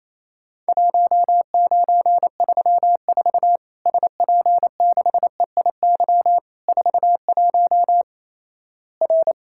Morse code, heard as a keyed tone, about 28 wpm, 700 Hz. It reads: 1934 SP6EIY 41 R
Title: Worked All Europe DX Contest - QTC Exchange - 28wpm